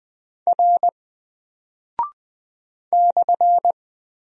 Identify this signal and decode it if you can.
Morse code, heard as a keyed tone, about 20 wpm, 700 Hz. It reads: R /
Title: Single Character (Letter:Number:Punctuation:Prosign) 20wpm